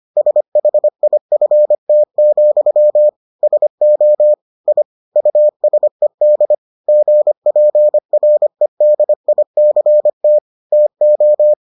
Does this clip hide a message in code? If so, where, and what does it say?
Yes: Morse code from 0.2 to 11.5 s, heard as a keyed tone, about 25 wpm, 600 Hz. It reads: SHIFT, SO I USED GPREDICT TO